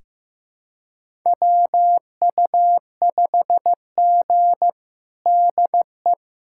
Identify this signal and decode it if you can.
Morse code, heard as a keyed tone, about 15 wpm, 700 Hz. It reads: WU5G DE
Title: USA Callsign DE Callsign - 4 Characters - 15wpm